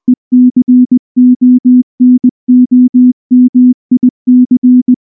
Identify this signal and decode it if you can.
Morse code, heard as a keyed tone, about 19 wpm, 250 Hz. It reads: ECONOMIC
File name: morse-training-20WPM